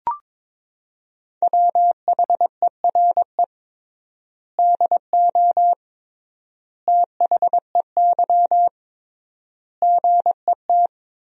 Morse code, heard as a keyed tone, about 21 wpm, 700 Hz. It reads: WHERE DO THEY GET
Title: 3x Word-Spacing - Sentences from Top 200 Words - 22wpm